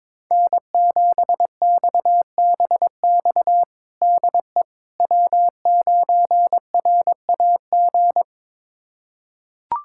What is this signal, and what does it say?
Morse code, heard as a keyed tone, about 22 wpm, 700 Hz. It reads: N7XBX DE W9RAG
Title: USA Callsign DE Callsign - 5 Characters - 22wpm